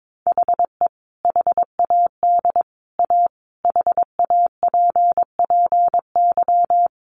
Morse code, heard as a keyed tone, about 22 wpm, 700 Hz. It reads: HE HAD A HAPPY
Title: Sentences from Top 1000 Words 22wpm